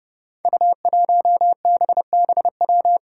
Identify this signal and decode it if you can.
Morse code, heard as a keyed tone, about 30 wpm, 700 Hz. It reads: U1BBW
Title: Five Challenging Letter-Number Combinations 30wpm